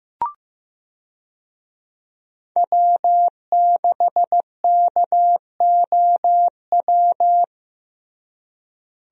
Morse code, heard as a keyed tone, about 15 wpm, 700 Hz. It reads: W6KOW
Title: USA Call Signs - 5 Characters - Repeated 2x - 15wpm